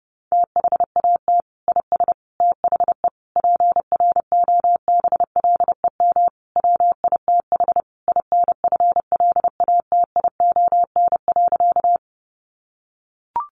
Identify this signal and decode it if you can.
Morse code, heard as a keyed tone, about 28 wpm, 700 Hz. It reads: THAT IS THE PROBLEM WITH INFLATION.
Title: Sentences from Top 1100 Words 30wpm